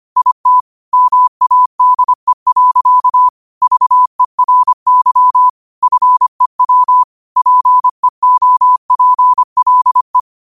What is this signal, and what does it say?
Morse code, heard as a keyed tone, about 25 wpm, 1000 Hz. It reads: IT MADE. VERY FEW PEOPLE
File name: qotdfiles / 25wpm